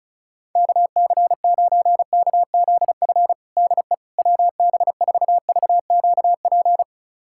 Morse code, heard as a keyed tone, about 34 wpm, 700 Hz. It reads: KC9KZF DE WB4VQP
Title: USA Callsign DE Callsign - 6 Characters - 35wpm